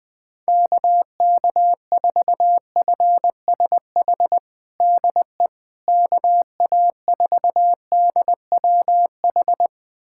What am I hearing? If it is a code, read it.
Morse code, heard as a keyed tone, about 20 wpm, 700 Hz. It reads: KK4FSH DE KA4DWH
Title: USA Callsign DE Callsign - 6 Characters - 20wpm